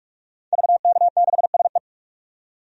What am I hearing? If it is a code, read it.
Morse code, heard as a keyed tone, about 46 wpm, 700 Hz. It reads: VK6SE
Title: DXCC-100 Call Signs - 5 Characters - 45wpm